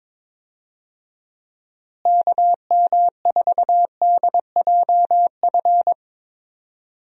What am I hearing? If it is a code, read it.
Morse code, heard as a keyed tone, about 22 wpm, 700 Hz. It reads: KM4DJF
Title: USA Call Signs - 6 Characters - Repeated 2x - 22wpm